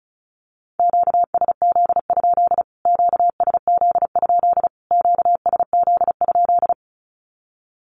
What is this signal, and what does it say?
Morse code, heard as a keyed tone, about 34 wpm, 700 Hz. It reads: QSZ? QSZ? QSZ?
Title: Q-Codes Repeated 3x 35wpm